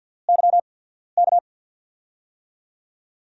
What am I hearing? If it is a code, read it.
Morse code, heard as a keyed tone, about 47 wpm, spaced as for 13 wpm, 700 Hz. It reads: YK
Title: USA Call Signs - 6 Characters - 8x Character Spacing - 50wpm